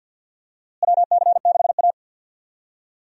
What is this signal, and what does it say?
Morse code, heard as a keyed tone, about 47 wpm, 700 Hz. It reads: WC6A
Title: USA Call Signs - 4 Characters - Repeated 3x - 50wpm